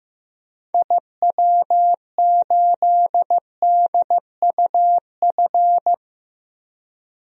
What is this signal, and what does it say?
Morse code, heard as a keyed tone, about 15 wpm, 700 Hz. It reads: IW8DUF
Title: DXCC-100 Call Signs - 6 Characters - Repeated 2x - 15wpm